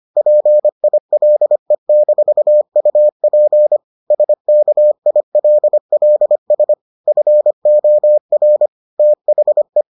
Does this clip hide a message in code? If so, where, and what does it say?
Yes: Morse code from 0.2 to 9.8 s, heard as a keyed tone, about 25 wpm, 600 Hz. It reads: PILE-UP SKILLS FOR THE